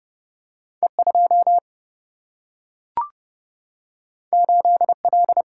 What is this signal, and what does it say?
Morse code, heard as a keyed tone, about 30 wpm, 700 Hz. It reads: E2 8L
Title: Single-Letter Single-Number 30wpm